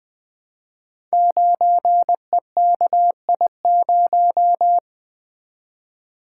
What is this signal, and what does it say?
Morse code, heard as a keyed tone, about 20 wpm, 700 Hz. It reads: 9EKI0
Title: Code Groups - 5 Letter:Numbers - 20wpm